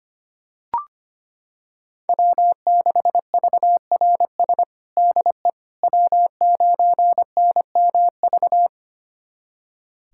Morse code, heard as a keyed tone, about 25 wpm, 700 Hz. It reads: W6VRS DE W9NMV
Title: USA Callsign DE Callsign - 5 Characters - 25wpm